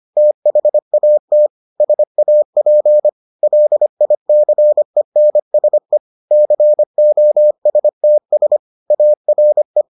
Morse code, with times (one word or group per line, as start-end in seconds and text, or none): 0.2-1.5 THAT
1.8-3.1 SAP
3.4-6.0 LICENSE
6.3-8.6 COSTS
8.9-9.8 ARE